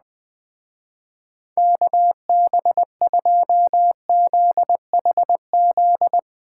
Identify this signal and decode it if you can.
Morse code, heard as a keyed tone, about 20 wpm, 700 Hz. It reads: KB2ZHZ
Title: Speed Racing - USA Call Signs 17wpm